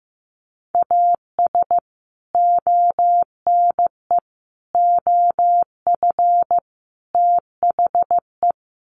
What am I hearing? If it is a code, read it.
Morse code, heard as a keyed tone, about 15 wpm, 700 Hz. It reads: AS ONE OF THE
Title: Sets of 4 Words 15wpm